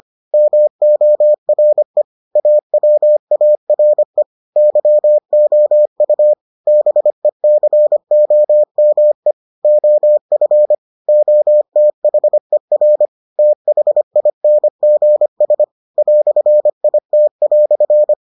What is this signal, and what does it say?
Morse code, heard as a keyed tone, about 25 wpm, 600 Hz. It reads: MORE AWARE YOU BECOME OF OTHER THINGS "IT"